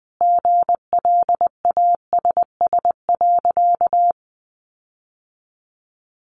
Morse code, heard as a keyed tone, about 20 wpm, 700 Hz. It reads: GLASS.
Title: Sentences from Top 1300 Words 20wpm